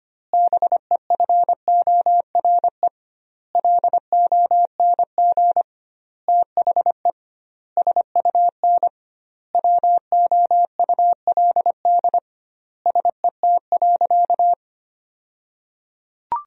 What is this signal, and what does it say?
Morse code, heard as a keyed tone, about 25 wpm, 700 Hz. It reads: BEFORE LONG THE SUN WOULD SET.
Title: 2x Word-Spacing - Sentences from Top 400 Words - 25wpm